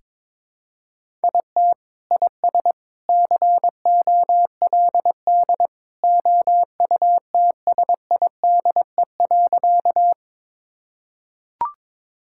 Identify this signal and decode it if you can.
Morse code, heard as a keyed tone, about 22 wpm, 700 Hz. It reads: IT IS COLD OUTSIDE.
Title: Sentences from Top 1000 Words - Review Part 6 - 22wpm